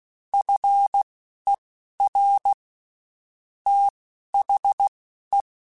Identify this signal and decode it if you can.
Morse code, heard as a keyed tone, about 16 wpm, spaced as for 11 wpm, 800 Hz. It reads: FER THE
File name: (spaced) / QSOs Spaced-mp3s / kc7haa @15 sp